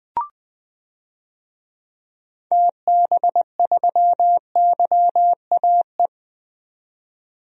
Morse code, heard as a keyed tone, about 20 wpm, 700 Hz. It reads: TB3YAE
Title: DXCC-100 Call Signs - 6 Characters - 20wpm